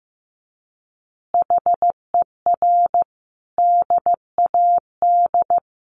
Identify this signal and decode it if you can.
Morse code, heard as a keyed tone, about 15 wpm, 700 Hz. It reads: HER DAD
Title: Additional Easy Phrases 15wpm